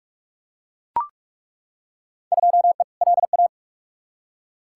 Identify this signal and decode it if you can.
Morse code, heard as a keyed tone, about 41 wpm, 700 Hz. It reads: 2E LA
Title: Field Day Exchanges Encore - 45wpm